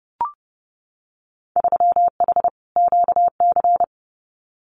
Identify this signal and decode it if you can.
Morse code, heard as a keyed tone, about 30 wpm, 700 Hz. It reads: 3H QC
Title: Winter Field Day Exchanges 30wpm